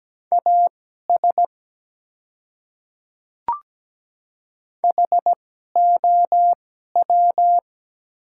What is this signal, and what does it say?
Morse code, heard as a keyed tone, about 17 wpm, spaced as for 12 wpm, 700 Hz. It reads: AS HOW
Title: 2x Character Spacing - Top 100 Words - 17wpm